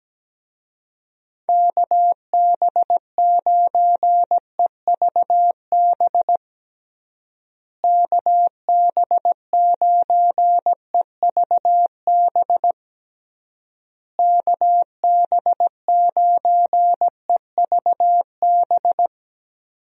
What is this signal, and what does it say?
Morse code, heard as a keyed tone, about 17 wpm, 700 Hz. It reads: KB9EVB KB9EVB KB9EVB
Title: USA Call Signs - 6 Characters - Repeated 3x - 17wpm